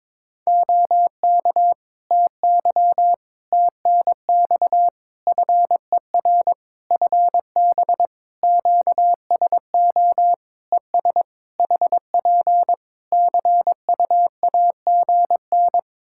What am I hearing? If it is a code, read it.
Morse code, heard as a keyed tone, about 22 wpm, 700 Hz. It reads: OK TY TNX FER FB QSO ES HP CUAGN
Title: K1USN SST QSO Practice 22wpm